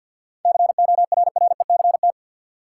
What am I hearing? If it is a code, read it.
Morse code, heard as a keyed tone, about 53 wpm, 700 Hz. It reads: CORRECT
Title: Sentences from Top 700 Words - New Word Drill - 50wpm